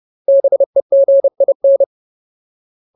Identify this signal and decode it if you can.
Morse code, heard as a keyed tone, about 30 wpm, 550 Hz. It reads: BEGIN
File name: Ultimate Word List / 30WPM Ultimate